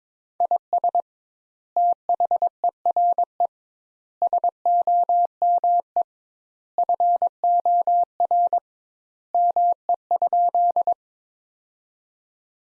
Morse code, heard as a keyed tone, about 22 wpm, 700 Hz. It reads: IS THERE SOME FOR ME?
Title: Sentences Top 200 Words - 22wpm